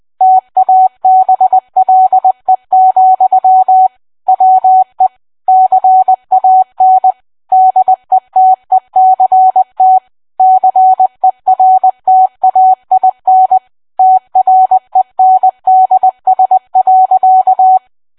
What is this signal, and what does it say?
Morse code, heard as a keyed tone, about 20 wpm, 750 Hz. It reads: TABLE, WE CAN DETECT CERTAIN TRENDS.